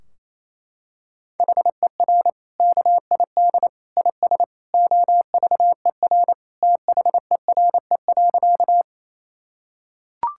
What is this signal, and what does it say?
Morse code, heard as a keyed tone, about 27 wpm, 700 Hz. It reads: HER KID IS OVER THERE.
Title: Sentences from Top 1600 Words 28wpm